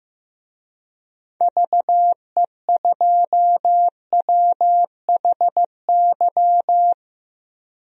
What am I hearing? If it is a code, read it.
Morse code, heard as a keyed tone, about 15 wpm, 700 Hz. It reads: VE2WHY
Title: DXCC-100 Call Signs - 6 Characters - Repeated 3x - 15wpm